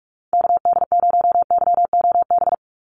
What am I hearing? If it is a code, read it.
Morse code, heard as a keyed tone, about 46 wpm, 700 Hz. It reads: KD0YOB
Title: NA SOTA Activator Callsigns 45wpm